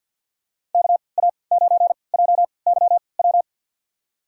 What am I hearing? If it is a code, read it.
Morse code, heard as a keyed tone, about 53 wpm, spaced as for 29 wpm, 700 Hz. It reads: KA9JYW
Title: USA Call Signs - 3x Character Spacing - 50wpm